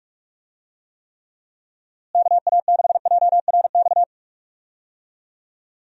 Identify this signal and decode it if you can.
Morse code, heard as a keyed tone, about 41 wpm, 700 Hz. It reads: KA6JRX
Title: USA Call Signs - 6 Characters - Repeated 3x - 45wpm